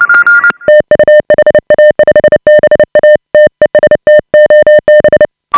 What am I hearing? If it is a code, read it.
Morse code, heard as a keyed tone, about 29 wpm, 590 Hz. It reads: TU HA5BA TEST OB